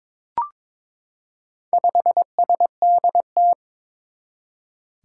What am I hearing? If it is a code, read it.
Morse code, heard as a keyed tone, about 22 wpm, 700 Hz. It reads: 5SDT
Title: Four Challenging Letter-Number Combinations 22wpm